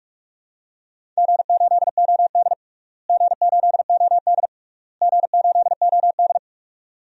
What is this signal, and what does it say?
Morse code, heard as a keyed tone, about 43 wpm, 700 Hz. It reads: G8OD G8OD G8OD
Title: DXCC-100 Call Signs - 4 Characters - Repeated 3x - 45wpm